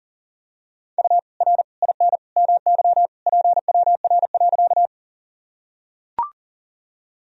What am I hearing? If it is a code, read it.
Morse code, heard as a keyed tone, about 40 wpm, 700 Hz. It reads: U R IN MY PWR.
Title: Sentences from Top 500 Words and Abbreviations 40wpm